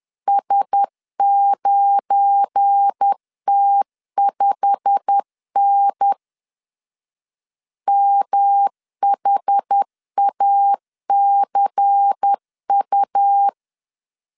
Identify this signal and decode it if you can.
Morse code, heard as a keyed tone, about 10 wpm, 800 Hz. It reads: S9T5N MHACU
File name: LEZIONI / 8 YJ45 / 8 YJ45 10wpm 2549